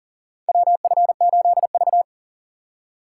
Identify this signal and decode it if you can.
Morse code, heard as a keyed tone, about 40 wpm, 700 Hz. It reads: WF8V
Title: USA Call Signs - 4 Characters - 4x Character Spacing - 40wpm